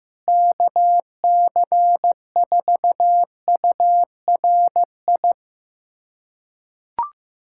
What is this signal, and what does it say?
Morse code, heard as a keyed tone, about 15 wpm, 700 Hz. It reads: KC4URI
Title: USA Callsign DE Callsign 15wpm